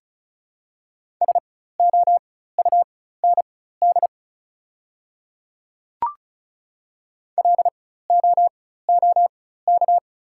Morse code, heard as a keyed tone, about 36 wpm, spaced as for 16 wpm, 700 Hz. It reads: SOUND LOOK
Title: 4x Character Spacing - Top 100 Words - 35wpm